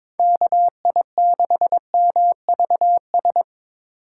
Morse code, heard as a keyed tone, about 25 wpm, 700 Hz. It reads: KI6MVS
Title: Speed Racing - USA 2x3 Call Signs 22wpm